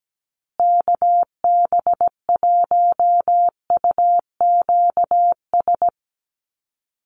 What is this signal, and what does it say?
Morse code, heard as a keyed tone, about 17 wpm, 700 Hz. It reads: KB1UQS
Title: USA Call Signs Repeated 3x 17wpm